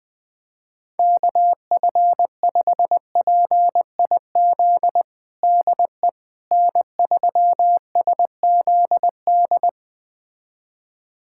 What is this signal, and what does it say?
Morse code, heard as a keyed tone, about 21 wpm, 700 Hz. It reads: KF5PIZ DE N3SZD
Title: USA Callsign DE Callsign 20wpm